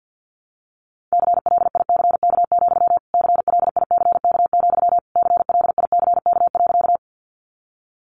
Morse code, heard as a keyed tone, about 50 wpm, 700 Hz. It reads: CLICK, CLICK, CLICK.
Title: Sentences from Top 1100 Words 50wpm